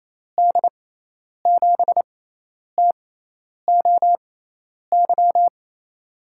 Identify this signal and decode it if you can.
Morse code, heard as a keyed tone, about 28 wpm, spaced as for 10 wpm, 700 Hz. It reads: D7TOY
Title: USA Call Signs - 6 Characters - 6x Character Spacing - 28wpm